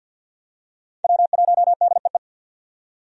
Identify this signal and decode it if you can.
Morse code, heard as a keyed tone, about 50 wpm, 700 Hz. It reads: W1DEE
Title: USA Call Signs - 5 Characters - 50wpm